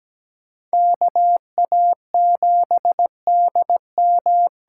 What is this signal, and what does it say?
Morse code, heard as a keyed tone, about 17 wpm, 700 Hz. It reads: KA7DM
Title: USA Call Signs - 6 Characters - 8x Character Spacing - 17wpm